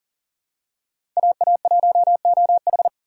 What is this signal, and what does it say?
Morse code, heard as a keyed tone, about 40 wpm, 690 Hz. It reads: AA1OH
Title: USA Call Signs - 4x Character Spacing - 40wpm